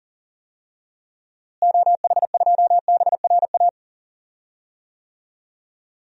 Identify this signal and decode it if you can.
Morse code, heard as a keyed tone, about 37 wpm, 700 Hz. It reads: OH2BRA
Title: DXCC-100 Call Signs - 6 Characters - Repeated 2x - 40wpm